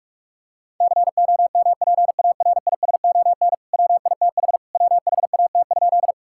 Morse code, heard as a keyed tone, about 46 wpm, 700 Hz. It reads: COMPARISON WITH WHAT?
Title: Sentences from Top 1600 Words 45wpm